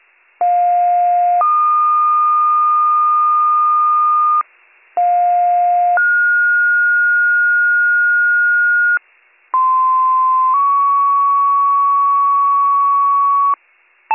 Two-tone page, heard as a one-second tone, then a three-second tone, about 710 then 1220 Hz, about 710 then 1430 Hz, about 1040 then 1120 Hz. It is heard from 0.4 to 13.5 s.